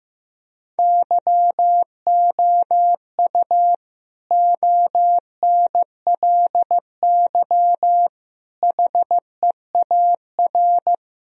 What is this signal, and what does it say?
Morse code, heard as a keyed tone, about 16 wpm, 700 Hz. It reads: YOU ONLY HEAR
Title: Sentences from Top 1000 Words - Review Part 2 - 15wpm